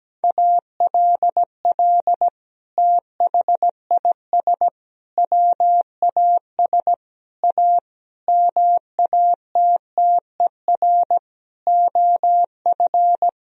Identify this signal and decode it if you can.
Morse code, heard as a keyed tone, about 17 wpm, 700 Hz. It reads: ALL THIS WAS A MATTER OF
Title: Sentences from Top 1000 Words - Review Part 4 - 17wpm